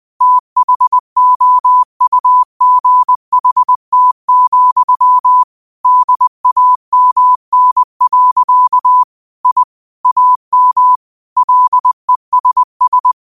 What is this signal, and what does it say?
Morse code, heard as a keyed tone, about 20 wpm, 1000 Hz. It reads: THOUGHT, DAMN. I AM LESS
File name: qotdfiles / 20wpm